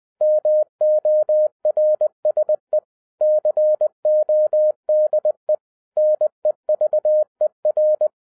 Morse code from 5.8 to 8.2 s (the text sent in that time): NEVER